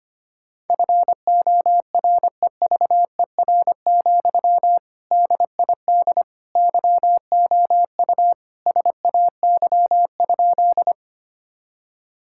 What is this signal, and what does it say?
Morse code, heard as a keyed tone, about 25 wpm, 700 Hz. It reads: FOREVER, DID YOU SAY?
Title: Sentences from Top 1600 Words 25wpm